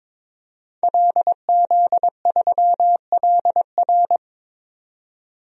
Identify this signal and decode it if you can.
Morse code, heard as a keyed tone, about 22 wpm, 700 Hz. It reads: LZ3LR
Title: DXCC-100 Call Signs - 5 Characters - 22wpm